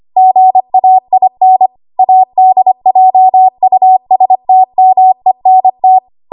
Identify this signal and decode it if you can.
Morse code, heard as a keyed tone, about 25 wpm, 750 Hz. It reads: GAIN ADJUSTMENT
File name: Archive / 25 WPM / 25WPM